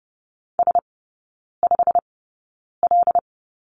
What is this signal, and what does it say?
Morse code, heard as a keyed tone, about 30 wpm, spaced as for 9 wpm, 700 Hz. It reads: S5L
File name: Lesson 16e - Letter c - Three Characters 30wpm